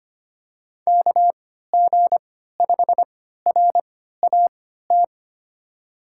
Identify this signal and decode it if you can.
Morse code, heard as a keyed tone, about 25 wpm, spaced as for 14 wpm, 700 Hz. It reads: KG5RAT
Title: USA Call Signs - 6 Characters - 3x Character Spacing - 25wpm